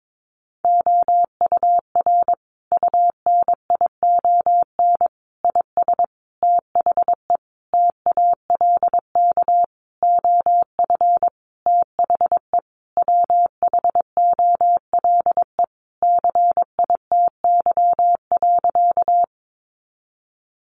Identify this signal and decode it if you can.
Morse code, heard as a keyed tone, about 22 wpm, 700 Hz. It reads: OUR UNION IS THE TALK OF THE WHOLE CITY.
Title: Sentences from Top 1400 Words 22wpm